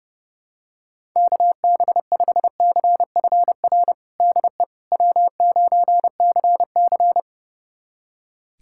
Morse code, heard as a keyed tone, about 31 wpm, 700 Hz. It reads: KB5CFR DE W9CC
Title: USA Callsign DE Callsign 30wpm